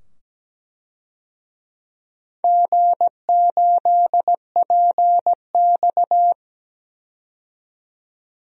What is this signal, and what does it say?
Morse code, heard as a keyed tone, about 17 wpm, 700 Hz. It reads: G8PX
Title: DXCC-100 Call Signs - 4 Characters - 17wpm